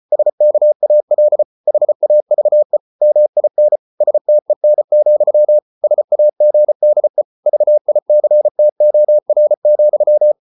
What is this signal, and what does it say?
Morse code, heard as a keyed tone, about 34 wpm, 600 Hz. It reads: SKAL HAVE MIN STEN, SAGDE VICTOR,
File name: Ferie QRQ 34wpm